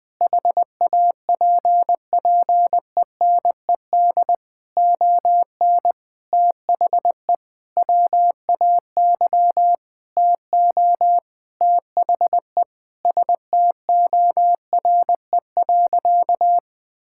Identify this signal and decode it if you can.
Morse code, heard as a keyed tone, about 19 wpm, 700 Hz. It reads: HAPPENED ON THE WAY TO THE STORE.
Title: Sentences with 9-Word Phrases - 20wpm